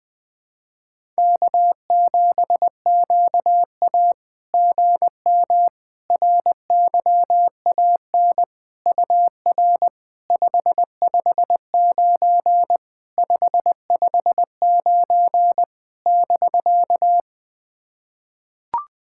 Morse code, heard as a keyed tone, about 21 wpm, 690 Hz. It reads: K7QA GM RYAN UR 559 559 <BK>
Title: SOTA QSO Practice 20wpm